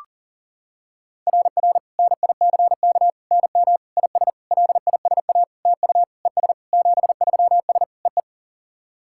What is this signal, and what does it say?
Morse code, heard as a keyed tone, about 40 wpm, 700 Hz. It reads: RR NICK NM IS LISA TU ES 73S EE
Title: SOTA QSO Practice 40wpm